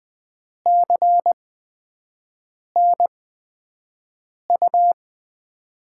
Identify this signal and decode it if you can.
Morse code, heard as a keyed tone, about 20 wpm, spaced as for 5 wpm, 700 Hz. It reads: CN U
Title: Top 90 QSO Elements - 8x Character Spacing - 20wpm